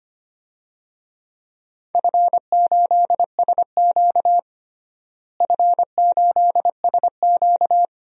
Morse code, heard as a keyed tone, about 25 wpm, 700 Hz. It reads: F8SQ F8SQ
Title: DXCC-100 Call Signs - 4 Characters - Repeated 2x - 25wpm